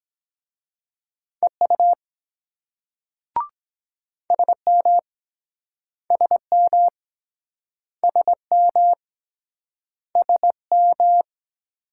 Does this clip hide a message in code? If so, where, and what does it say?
Yes: Morse code from 1.4 to 11.2 s, heard as a keyed tone, about 24 wpm, 700 Hz. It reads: EU SM SM SM SM